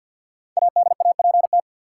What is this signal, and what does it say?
Morse code, heard as a keyed tone, about 52 wpm, 700 Hz. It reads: ADAPT
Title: Five Letter Words - 50wpm